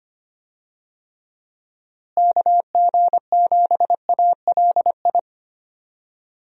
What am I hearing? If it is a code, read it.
Morse code, heard as a keyed tone, about 25 wpm, 700 Hz. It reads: KG7ALI
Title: USA Call Signs - 6 Characters - Repeated 2x - 25wpm